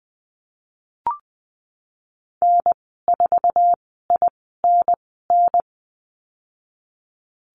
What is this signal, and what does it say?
Morse code, heard as a keyed tone, about 20 wpm, spaced as for 15 wpm, 700 Hz. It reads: N4INN
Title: 082 Lesson 14g - Number 2 - Call Signs 20wpm